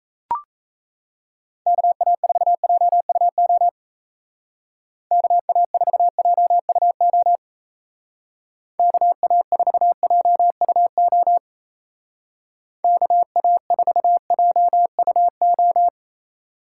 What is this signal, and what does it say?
Morse code, heard as a keyed tone, about 34 wpm, 700 Hz. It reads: KA4JUO KA4JUO KA4JUO KA4JUO
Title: Speed Racing - USA 2x3 Call Signs 28wpm